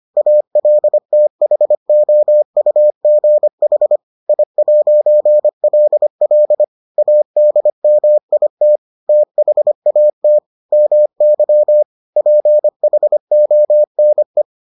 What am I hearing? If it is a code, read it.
Morse code, heard as a keyed tone, about 25 wpm, 600 Hz. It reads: ALTHOUGH I'LL ADMIT THAT MY PHONE